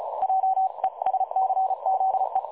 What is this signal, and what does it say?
Morse code, heard as a keyed tone, about 33 wpm, 760 Hz. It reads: JESCZE
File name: dlazaawansowanych / nagraniazpasma / websdr recording 3565.0kHz